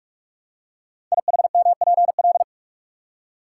Morse code, heard as a keyed tone, about 44 wpm, 700 Hz. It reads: IHMPL
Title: Code Groups - 5 Letters - 45wpm